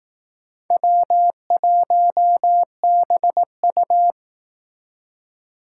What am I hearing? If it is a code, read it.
Morse code, heard as a keyed tone, about 18 wpm, 700 Hz. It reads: W1BU
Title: Speed Racing - USA 1x2 Call Signs 15wpm